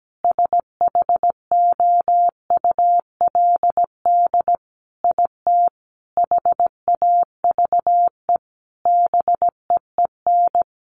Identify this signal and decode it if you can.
Morse code, heard as a keyed tone, about 17 wpm, 700 Hz. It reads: SHOULD IT HAVE BEEN
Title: Sentences from Top 1200 Words 17wpm